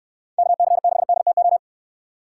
Morse code, heard as a keyed tone, about 69 wpm, 700 Hz. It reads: KC6DTO